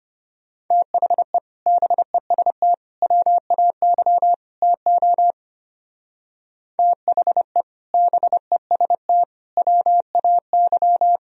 Morse code, heard as a keyed tone, about 30 wpm, 700 Hz. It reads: THE BEST WAY TO THE BEST WAY
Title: Speed-Racing - Sets of 4 Words 25wpm